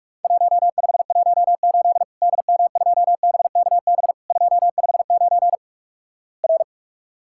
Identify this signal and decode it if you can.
Morse code, heard as a keyed tone, about 46 wpm, 700 Hz. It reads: 1518 DM2BKB 259 R
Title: Worked All Europe DX Contest - QTC Exchange - 45wpm